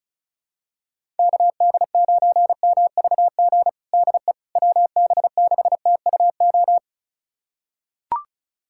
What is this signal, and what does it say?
Morse code, heard as a keyed tone, about 34 wpm, 700 Hz. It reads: KD9MVG DE WB6TUO
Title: USA Callsign DE Callsign 35wpm